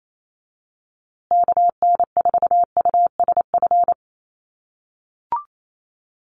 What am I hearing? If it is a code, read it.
Morse code, heard as a keyed tone, about 28 wpm, 700 Hz. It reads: KN4USF
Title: USA Call Signs Repeated 2x 28wpm